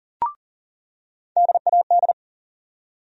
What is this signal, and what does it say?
Morse code, heard as a keyed tone, about 40 wpm, 700 Hz. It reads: DAD
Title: USA Call Sign Postfixes - 3 Letters 40wpm